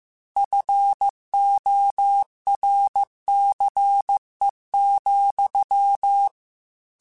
Morse code, heard as a keyed tone, about 15 wpm, 800 Hz. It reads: FORCE,
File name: Technical Subjects / Technical Subjects-mp3s / XL @13